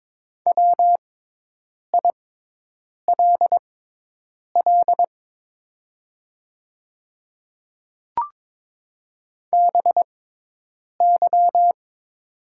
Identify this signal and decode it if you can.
Morse code, heard as a keyed tone, about 22 wpm, spaced as for 8 wpm, 700 Hz. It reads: WILL BY
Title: 6x Character Spacing - Top 200 Words - 22wpm